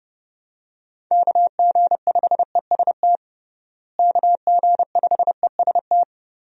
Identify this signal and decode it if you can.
Morse code, heard as a keyed tone, about 30 wpm, 700 Hz. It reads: KG5EST KG5EST
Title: USA Call Signs - 6 Characters - Repeated 3x - 30wpm